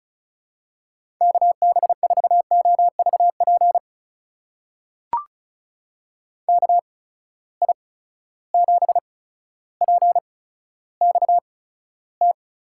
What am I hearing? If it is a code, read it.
Morse code, heard as a keyed tone, about 36 wpm, spaced as for 15 wpm, 700 Hz. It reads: KB4OVP KI7PXT
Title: USA Call Signs - 6 Characters - 8x Character Spacing - 35wpm